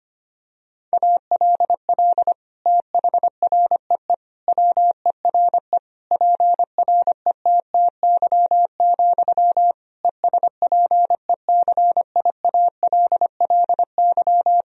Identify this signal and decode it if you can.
Morse code, heard as a keyed tone, about 25 wpm, 700 Hz. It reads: ALL THREE WERE PRETTY, ESPECIALLY
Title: Sentences from Top 1000 Words - Review Part 4 - 25wpm